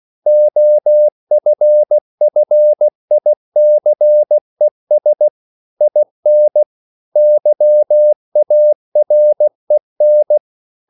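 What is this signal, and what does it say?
Morse code, heard as a keyed tone, about 16 wpm, 600 Hz. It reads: OFFICES IN YAREN
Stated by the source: CW practice of national capitals at 16 words per minute, from ON6ZQ.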